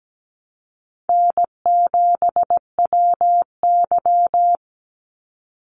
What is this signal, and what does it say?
Morse code, heard as a keyed tone, about 17 wpm, 700 Hz. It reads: N7WY
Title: Easy Call Signs - 17wpm